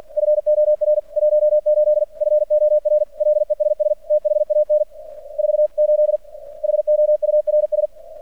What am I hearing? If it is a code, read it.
Morse code, heard as a keyed tone, about 46 wpm, 600 Hz. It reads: FOR 18 FOR LESS THAN 48 HOURS
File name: archived / CRSnet